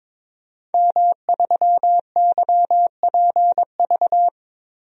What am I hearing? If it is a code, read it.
Morse code, heard as a keyed tone, about 22 wpm, 700 Hz. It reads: M3YPV
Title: DXCC-100 Call Signs - 5 Characters - Repeated 2x - 22wpm